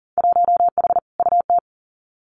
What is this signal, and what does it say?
Morse code, heard as a keyed tone, about 40 wpm, 700 Hz. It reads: 1H UT
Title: Winter Field Day Exchanges 40wpm